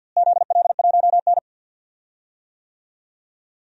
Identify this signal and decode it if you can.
Morse code, heard as a keyed tone, about 52 wpm, 700 Hz. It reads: ZL1N
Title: DXCC-100 Call Signs - 4 Characters - Repeated 3x - 50wpm